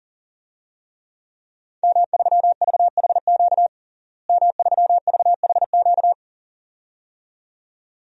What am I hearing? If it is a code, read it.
Morse code, heard as a keyed tone, about 40 wpm, 700 Hz. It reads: M3VHQ M3VHQ
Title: DXCC-100 Call Signs - 5 Characters - Repeated 2x - 40wpm